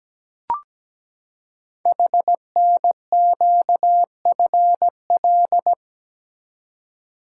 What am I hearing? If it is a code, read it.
Morse code, heard as a keyed tone, about 17 wpm, 700 Hz. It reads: HNQFL
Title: Code Groups - 5 Letters - 17wpm